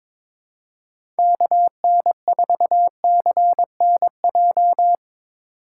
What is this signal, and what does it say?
Morse code, heard as a keyed tone, about 22 wpm, 700 Hz. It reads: KN4CNJ
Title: USA Call Signs - 8x Character Spacing - 22wpm